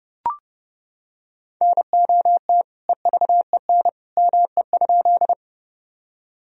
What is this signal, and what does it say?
Morse code, heard as a keyed tone, about 30 wpm, 700 Hz. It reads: NOT EVEN ME?
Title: Sentences from Top 1000 Words - Review Part 1 - 30wpm